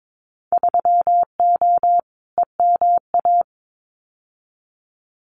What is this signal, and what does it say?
Morse code, heard as a keyed tone, about 22 wpm, 700 Hz. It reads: 3O EMA
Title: Winter Field Day Exchanges Encore 22wpm